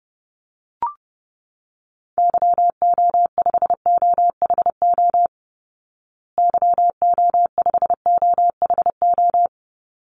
Morse code, heard as a keyed tone, about 30 wpm, 700 Hz. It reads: YO5OHO YO5OHO
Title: Call Signs Repeated 3x - 30wpm